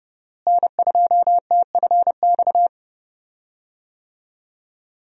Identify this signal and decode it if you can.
Morse code, heard as a keyed tone, about 30 wpm, 700 Hz. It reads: N2TFX
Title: USA Call Signs - 5 Characters - Repeated 2x - 30wpm